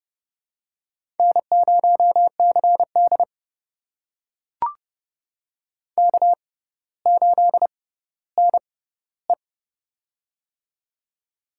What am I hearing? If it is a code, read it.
Morse code, heard as a keyed tone, about 30 wpm, spaced as for 15 wpm, 700 Hz. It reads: N0CD K8NE
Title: USA Call Signs - 4 Characters - 6x Character Spacing - 30wpm